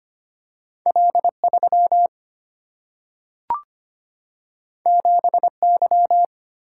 Morse code, heard as a keyed tone, about 25 wpm, 700 Hz. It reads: L3 7Y
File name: Two Letter-Number Combinations - 25wpm